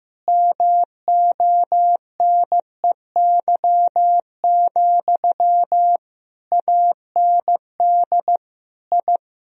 Morse code, heard as a keyed tone, about 15 wpm, 700 Hz. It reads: MONEY, AND I
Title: Sentences from Top 1000 Words - Review Part 2 - 15wpm